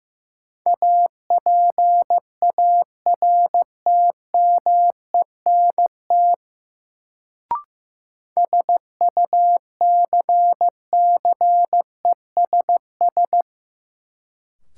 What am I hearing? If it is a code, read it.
Morse code, heard as a keyed tone, about 15 wpm, 700 Hz. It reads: APARTMENT SUCCESS
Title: Sentences from Top 900 Words - New Word Drill - 15wpm